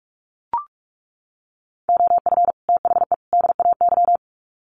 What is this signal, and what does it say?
Morse code, heard as a keyed tone, about 44 wpm, 700 Hz. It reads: OF THE DAY
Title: Sets of 3 Words 45wpm